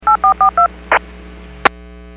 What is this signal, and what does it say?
DTMF, heard as telephone touch tones, tones of about 110 ms. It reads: 8442
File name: WPS / 03038 11-14 WPS 451.25 WPS